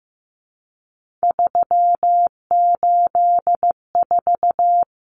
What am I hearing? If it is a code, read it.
Morse code, heard as a keyed tone, about 15 wpm, 700 Hz. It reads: 384
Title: Numbers - 3-Digit 15wpm